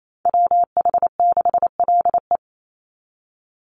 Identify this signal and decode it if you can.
Morse code, heard as a keyed tone, about 28 wpm, 700 Hz. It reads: WH6LE
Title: NA SOTA Activator Callsigns 28wpm